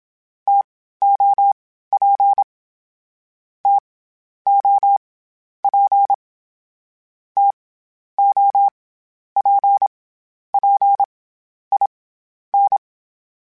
Morse code, heard as a keyed tone, about 27 wpm, 800 Hz. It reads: T O P T O P T O P P I N
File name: Variable Speed / Variable Speed-mp3s / TO sp @25